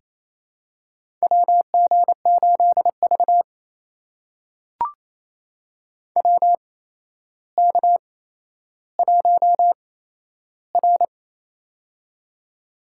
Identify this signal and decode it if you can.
Morse code, heard as a keyed tone, about 28 wpm, spaced as for 12 wpm, 700 Hz. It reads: WG8V WK1R
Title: USA Call Signs - 4 Characters - 8x Character Spacing - 28wpm